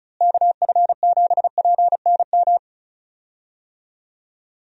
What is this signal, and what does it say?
Morse code, heard as a keyed tone, about 36 wpm, 700 Hz. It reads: KF7PNM
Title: USA Call Signs - 6 Characters - Repeated 2x - 35wpm